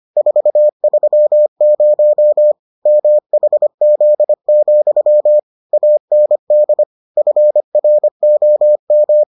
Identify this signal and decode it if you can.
Morse code, heard as a keyed tone, about 25 wpm, 600 Hz. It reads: 430 MHZ, AND FROM